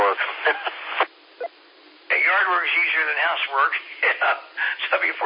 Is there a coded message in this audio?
Yes: Emergency Alert System (SAME) tones from 0.2 to 4.1 s, heard as harsh buzzing bursts, with speech.